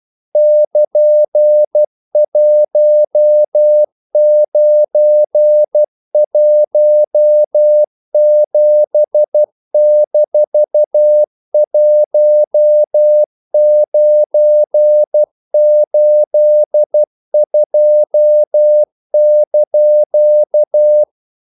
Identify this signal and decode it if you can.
Morse code, heard as a keyed tone, about 12 wpm, 600 Hz. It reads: <KN>1917-1982)
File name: CW / Practice / EnglishQuotations0312